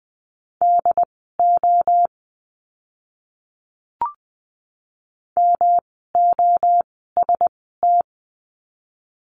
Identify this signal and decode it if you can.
Morse code, heard as a keyed tone, about 20 wpm, spaced as for 14 wpm, 700 Hz. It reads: DO MOST
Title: 139 Lesson 23f - Letter f - Words 20wpm